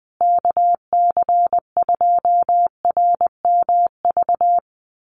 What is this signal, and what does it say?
Morse code, heard as a keyed tone, about 20 wpm, 700 Hz. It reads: KC2RMV
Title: USA Call Signs Repeated 3x 20wpm